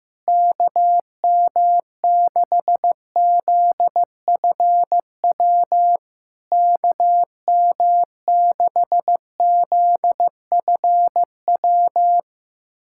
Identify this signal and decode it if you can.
Morse code, heard as a keyed tone, about 15 wpm, 700 Hz. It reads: KM6ZFW KM6ZFW
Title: SOTA QSO Practice 15wpm